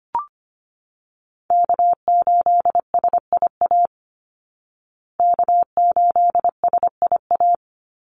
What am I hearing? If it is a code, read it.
Morse code, heard as a keyed tone, about 25 wpm, 700 Hz. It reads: K8SIA K8SIA
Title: Call Signs Repeated 3x - 25wpm